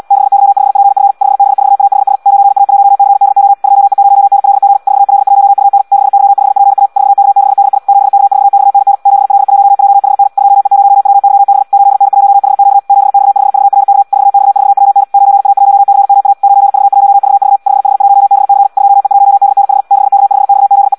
Morse code, heard as a keyed tone, about 78 wpm, 800 Hz. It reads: 6Y5PF 5B4AHS 9A1CDO 2E0NLG 4L1UN 6Y5PA 4K6DI 7K4XNN 7L1JHN 2E0WYB 8N1HQ 7X5VRK 4X4WN 9A1CMA 9V1VV 5H1BC 2E1DNH 6K5XZE
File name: Audio practica CW / qrq-tx-78 2